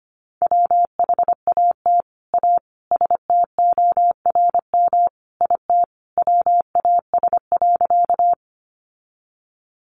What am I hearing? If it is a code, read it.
Morse code, heard as a keyed tone, about 25 wpm, 700 Hz. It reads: WHAT A STORM IT WAS.
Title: Sentences from Top 1500 Words 25wpm